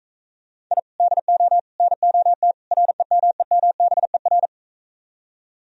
Morse code, heard as a keyed tone, about 45 wpm, 700 Hz. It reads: I DO NOT REMEMBER